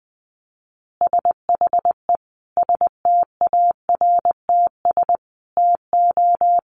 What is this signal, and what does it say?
Morse code, heard as a keyed tone, about 20 wpm, 700 Hz. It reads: SHE STARTS TO
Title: Easy Phrases 20wpm